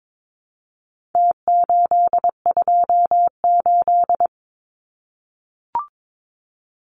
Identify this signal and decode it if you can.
Morse code, heard as a keyed tone, about 22 wpm, 700 Hz. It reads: T828
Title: Numbers - Contest Style - 4-Digit 22wpm